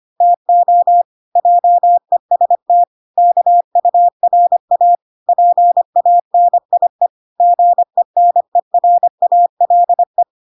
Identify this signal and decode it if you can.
Morse code, heard as a keyed tone, about 25 wpm, 700 Hz. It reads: TO JEST KURA PANIE GENERALE